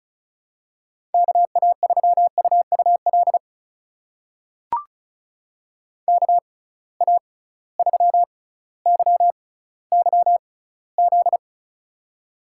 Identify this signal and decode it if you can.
Morse code, heard as a keyed tone, about 36 wpm, spaced as for 18 wpm, 700 Hz. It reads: KA3UUL KA3YYZ
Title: USA Call Signs - 6 Characters - 6x Character Spacing - 35wpm